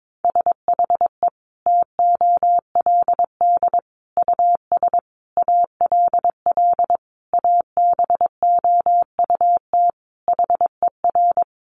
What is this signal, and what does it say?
Morse code, heard as a keyed tone, about 22 wpm, 700 Hz. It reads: SHE TOLD US ALL ABOUT HER
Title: Sentences from Top 1200 Words 22wpm